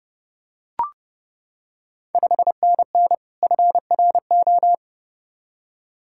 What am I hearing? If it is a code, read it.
Morse code, heard as a keyed tone, about 30 wpm, 700 Hz. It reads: 5NN FRO
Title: Ontario QSO Party Exchange 30wpm